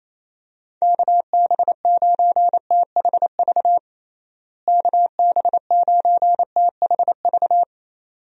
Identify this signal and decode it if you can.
Morse code, heard as a keyed tone, about 28 wpm, 700 Hz. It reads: KB9THV KB9THV
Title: USA Call Signs - 6 Characters - Repeated 2x - 28wpm